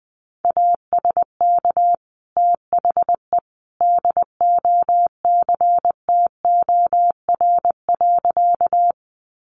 Morse code, heard as a keyed tone, about 20 wpm, 700 Hz. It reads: ASK THE DOCTOR.
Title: Sentences from Top 1500 Words 20wpm